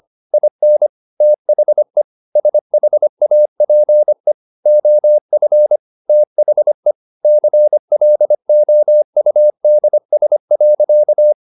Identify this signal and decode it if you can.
Morse code, heard as a keyed tone, about 25 wpm, 600 Hz. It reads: IN THE SHAPE OF THE CLOUDS.
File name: the old man and the sea part 1 25wpm